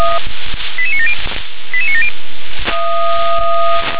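Five-tone selective call (ZVEI), heard as quick runs of five tones, tones of about 70 ms. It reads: 88970 88970
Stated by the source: ZVEI five-tone selective call 88977 received by SDR.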